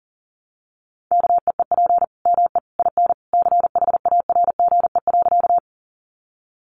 Morse code, heard as a keyed tone, about 40 wpm, 700 Hz. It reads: KEEP ME IN CHARGE.
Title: Sentences from Top 900 Words 40wpm